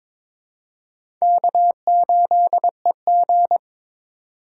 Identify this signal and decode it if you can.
Morse code, heard as a keyed tone, about 22 wpm, 700 Hz. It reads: K8EG
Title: USA Call Signs - 4 Characters - 4x Character Spacing - 22wpm